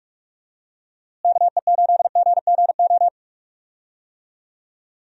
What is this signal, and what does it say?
Morse code, heard as a keyed tone, about 41 wpm, 700 Hz. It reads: KE8GGO
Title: USA Call Signs - 6 Characters - Repeated 2x - 45wpm